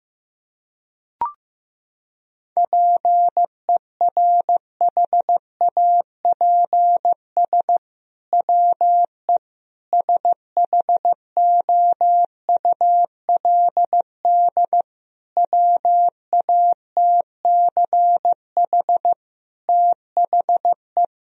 Morse code, heard as a keyed tone, about 15 wpm, 700 Hz. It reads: PERHAPS WE SHOULD WATCH THE
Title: Sentences from Top 1000 Words - Review Part 4 - 15wpm